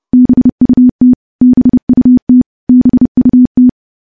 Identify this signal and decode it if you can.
Morse code, heard as a keyed tone, about 29 wpm, 250 Hz. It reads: BUT BUT BUT
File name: morse-training-30WPM